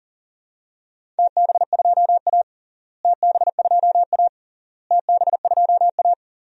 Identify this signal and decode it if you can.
Morse code, heard as a keyed tone, about 37 wpm, 700 Hz. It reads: TB2A TB2A TB2A
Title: DXCC-100 Call Signs - 4 Characters - Repeated 3x - 40wpm